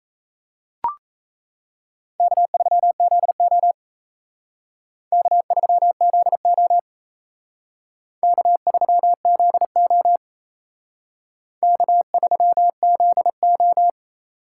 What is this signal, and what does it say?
Morse code, heard as a keyed tone, about 34 wpm, 700 Hz. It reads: K3ZO K3ZO K3ZO K3ZO
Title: Speed Racing - USA 1x2 Call Signs 28wpm